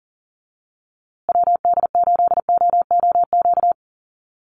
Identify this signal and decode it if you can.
Morse code, heard as a keyed tone, about 40 wpm, 700 Hz. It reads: WD8OOQ
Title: USA Call Signs Repeated 2x 40wpm